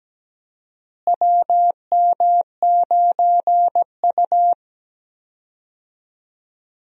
Morse code, heard as a keyed tone, about 17 wpm, 700 Hz. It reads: WM9U
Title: USA Call Signs - 4 Characters - Repeated 2x - 17wpm